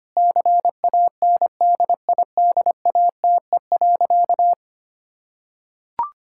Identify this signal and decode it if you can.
Morse code, heard as a keyed tone, about 25 wpm, 700 Hz. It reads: CANDIDATE.
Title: Sentences from Top 1600 Words 25wpm